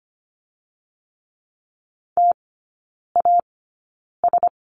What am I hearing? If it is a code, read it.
Morse code, heard as a keyed tone, about 25 wpm, spaced as for 9 wpm, 700 Hz. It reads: TAS
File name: course / 25:10wpm / 093 Lesson 16e - Letter c - Three Characters 25wpm